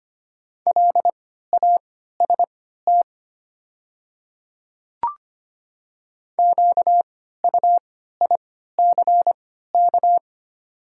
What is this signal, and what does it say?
Morse code, heard as a keyed tone, about 25 wpm, spaced as for 14 wpm, 700 Hz. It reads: LAST QUICK
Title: 3x Character Spacing - Top 500 Words - 25wpm